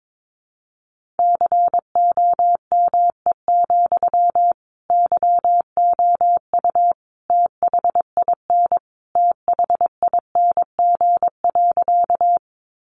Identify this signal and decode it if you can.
Morse code, heard as a keyed tone, about 22 wpm, 700 Hz. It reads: COME, YOU THIN THING.
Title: Sentences from Top 1200 Words 22wpm